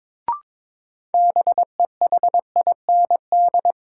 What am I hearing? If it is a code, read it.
Morse code, heard as a keyed tone, about 22 wpm, 700 Hz. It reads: BEHIND
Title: Sentences from Top 500 Words - New Word Drill - 22wpm